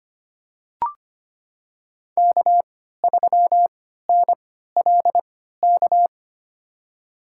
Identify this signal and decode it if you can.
Morse code, heard as a keyed tone, about 25 wpm, spaced as for 14 wpm, 700 Hz. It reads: K3NLK
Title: USA Call Signs - 3x Character Spacing - 25wpm